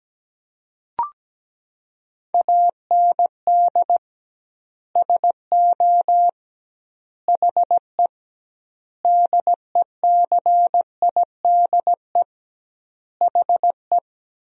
Morse code, heard as a keyed tone, about 17 wpm, 700 Hz. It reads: AND SO HE DECIDE HE
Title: 2x Word-Spacing - Sentences from Top 500 Words - 17wpm